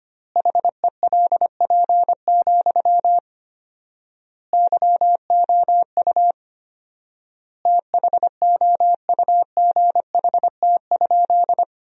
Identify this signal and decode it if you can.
Morse code, heard as a keyed tone, about 25 wpm, 700 Hz. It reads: HELP, YOU THOUGHT?
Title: 4x Word-Spacing - Sentences from Top 300 Words - 25wpm